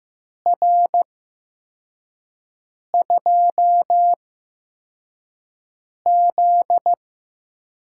Morse code, heard as a keyed tone, about 15 wpm, spaced as for 4 wpm, 700 Hz. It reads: R2Z
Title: USA Call Signs - 4 Characters - 8x Character Spacing - 15wpm